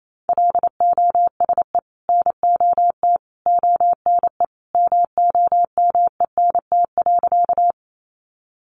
Morse code, heard as a keyed tone, about 28 wpm, 700 Hz. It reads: LOSE NOT ONE MOMENT.
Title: Sentences from Top 1100 Words 28wpm